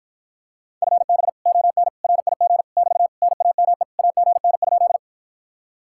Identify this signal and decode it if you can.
Morse code, heard as a keyed tone, about 53 wpm, 700 Hz. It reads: FB ON RIG <BT> NAME AGN?